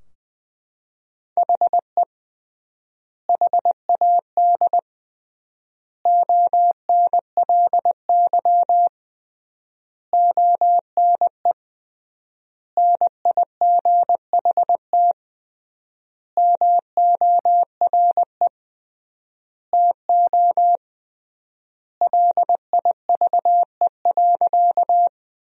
Morse code, heard as a keyed tone, about 21 wpm, 700 Hz. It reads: HE HAD ONLY ONE NIGHT MORE TO LIVE.
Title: 3x Word-Spacing - Sentences from Top 300 Words - 20wpm